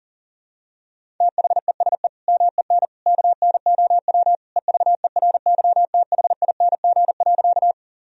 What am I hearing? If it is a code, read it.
Morse code, heard as a keyed tone, about 40 wpm, 700 Hz. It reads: THESE MEN KNOW EVERYTHING.
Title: Sentences from Top 1000 Words - Review Part 1 - 40wpm